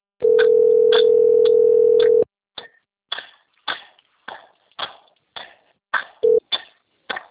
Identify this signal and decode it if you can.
Ringback tone (North America), 440 + 480 Hz.
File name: rg-1600-unknown- - - .38695